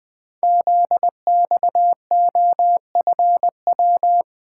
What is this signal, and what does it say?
Morse code, heard as a keyed tone, about 20 wpm, 700 Hz. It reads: ZXOFW
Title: Code Groups - 5 Letter:Numbers - 20wpm